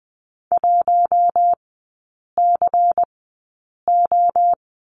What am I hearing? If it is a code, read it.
Morse code, heard as a keyed tone, about 20 wpm, spaced as for 8 wpm, 700 Hz. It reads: 1CO
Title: 152 Lesson 25e - Letter p - Three Characters 20wpm